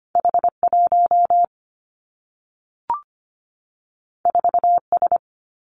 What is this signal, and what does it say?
Morse code, heard as a keyed tone, about 25 wpm, 700 Hz. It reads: H1 4S
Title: Two Challenging Letter-Number Combinations 25wpm